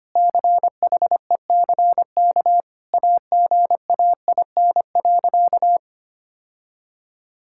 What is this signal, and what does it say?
Morse code, heard as a keyed tone, about 25 wpm, 700 Hz. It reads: CHECK AGAIN.
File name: practice / 25wpm / Sentences with 4-Word Phrases - 25wpm